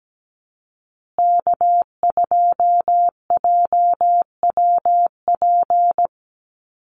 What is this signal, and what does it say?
Morse code, heard as a keyed tone, about 17 wpm, 700 Hz. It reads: K2JWP
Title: USA Call Signs Repeated 3x 17wpm